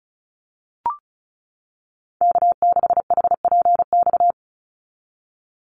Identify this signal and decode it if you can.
Morse code, heard as a keyed tone, about 34 wpm, 700 Hz. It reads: K6HPX
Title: NA SOTA Activator Callsigns 35wpm